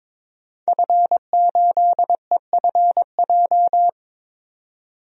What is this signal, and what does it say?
Morse code, heard as a keyed tone, about 22 wpm, 700 Hz. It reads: F8EFJ
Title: DXCC-100 Call Signs - 5 Characters - 22wpm